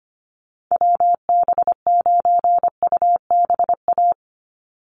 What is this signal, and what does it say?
Morse code, heard as a keyed tone, about 25 wpm, 700 Hz. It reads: WB9UBA
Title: USA Call Signs 25wpm